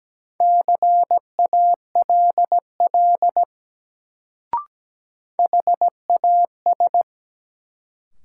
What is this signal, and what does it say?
Morse code, heard as a keyed tone, about 17 wpm, 700 Hz. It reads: CALL HAS
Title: Sentences from Top 100 Words - New Word Drill - 17wpm